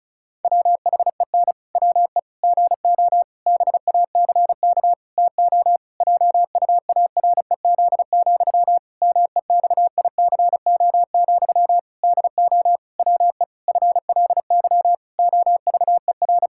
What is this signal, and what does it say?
Morse code, heard as a keyed tone, about 34 wpm, 700 Hz. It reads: WHEN WE GO BACK TO JUAREZ, MEXICO, DO WE FLY OVER
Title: Pangram Challenge v2 - 35wpm